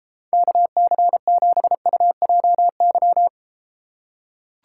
Morse code, heard as a keyed tone, about 33 wpm, 700 Hz. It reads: KC7UJY